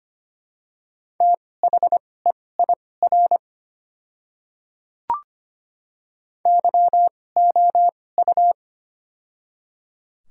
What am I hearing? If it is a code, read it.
Morse code, heard as a keyed tone, about 25 wpm, spaced as for 18 wpm, 690 Hz. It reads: THEIR YOU
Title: 2x Character Spacing - Top 100 Words - 25wpm